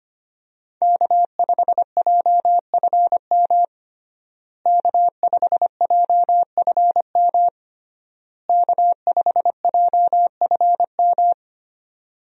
Morse code, heard as a keyed tone, about 25 wpm, 700 Hz. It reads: K5JFM K5JFM K5JFM
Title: USA Call Signs - 5 Characters - Repeated 3x - 25wpm